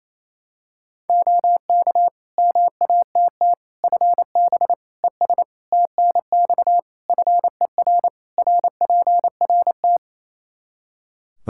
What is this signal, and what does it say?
Morse code, heard as a keyed tone, about 28 wpm, 700 Hz. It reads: OK MATT FB ES TNX FER RPRT
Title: Practice QSO - Exchange 2 - Info - 28wpm